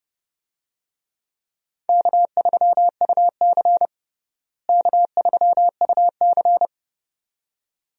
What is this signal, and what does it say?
Morse code, heard as a keyed tone, about 32 wpm, 700 Hz. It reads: K3UC K3UC
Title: USA Call Signs - 4 Characters - Repeated 2x - 30wpm